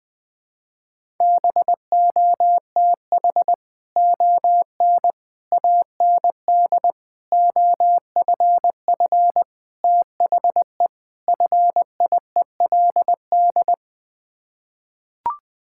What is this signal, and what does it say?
Morse code, heard as a keyed tone, about 20 wpm, 700 Hz. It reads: BOTH ON AND OFF THE FIELD
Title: Sets of 6 Words - 20wpm